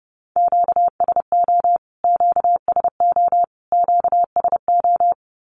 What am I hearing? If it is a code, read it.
Morse code, heard as a keyed tone, about 30 wpm, 700 Hz. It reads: QSO QSO QSO
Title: Q-Codes Repeated 3x 30wpm